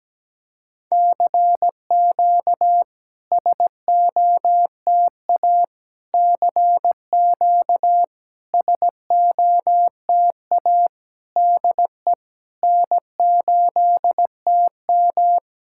Morse code, heard as a keyed tone, about 17 wpm, 700 Hz. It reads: CQ SOTA CQ SOTA DE N8TM
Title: SOTA QSO Practice 17wpm